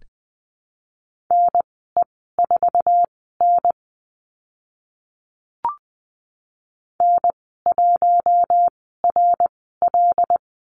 Morse code, heard as a keyed tone, about 20 wpm, spaced as for 14 wpm, 700 Hz. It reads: NE4N N1RL
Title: 076 Lesson 13g - Letter L - Call Signs 20wpm